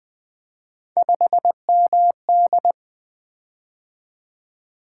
Morse code, heard as a keyed tone, about 21 wpm, 700 Hz. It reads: 5MD
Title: Three Letter-Number Combinations - 20wpm